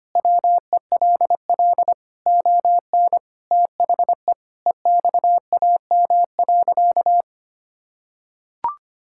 Morse code, heard as a keyed tone, about 25 wpm, 700 Hz. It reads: WELL ON THE EXAM.
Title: Sentences from Top 1000 Words - Review Part 5 - 25wpm